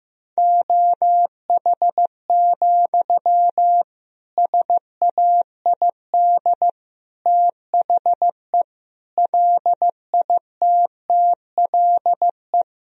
Morse code, heard as a keyed tone, about 15 wpm, 700 Hz. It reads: OH, SAID THE LITTLE
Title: Sentences from Top 1000 Words - Review Part 2 - 15wpm